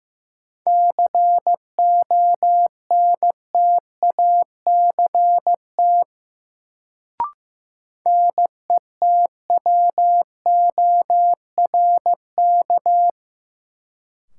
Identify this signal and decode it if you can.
Morse code, heard as a keyed tone, about 15 wpm, 700 Hz. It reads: CONTACT NETWORK
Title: Sentences from Top 1000 Words - Review Part 7 - New Word Drill - 15wpm